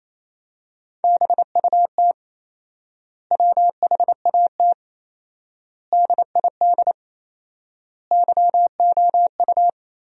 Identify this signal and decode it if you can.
Morse code, heard as a keyed tone, about 28 wpm, 700 Hz. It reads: BUT WHAT DID YOU
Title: 4x Word-Spacing - Sentences from Top 100 Words - 28wpm